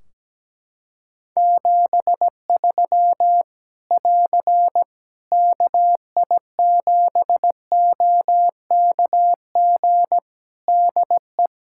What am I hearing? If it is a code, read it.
Morse code, heard as a keyed tone, about 17 wpm, 700 Hz. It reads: 73 <AR> KI7OKG DE
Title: K1USN SST QSO Practice 17wpm